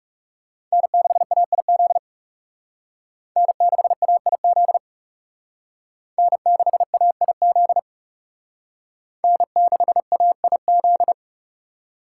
Morse code, heard as a keyed tone, about 39 wpm, 700 Hz. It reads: N6AIZ N6AIZ N6AIZ N6AIZ